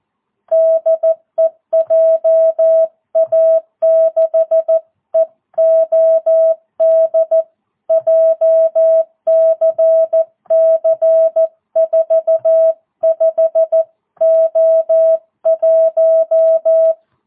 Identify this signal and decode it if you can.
Morse code, heard as a keyed tone, about 14 wpm, 650 Hz. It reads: DE JA6EOD JCC45O1
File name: jh8cbh / 2022cw